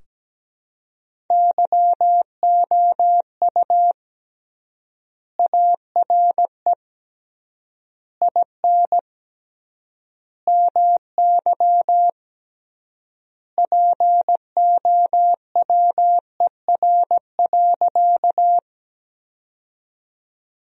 Morse code, heard as a keyed tone, about 17 wpm, 700 Hz. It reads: YOU ARE IN MY POWER.
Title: 3x Word-Spacing - Sentences from Top 500 Words - 17wpm